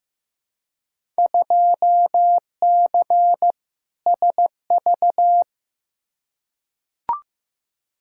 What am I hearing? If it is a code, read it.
Morse code, heard as a keyed tone, about 15 wpm, 700 Hz. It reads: 2C SV
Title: Field Day Exchanges Encore - 15wpm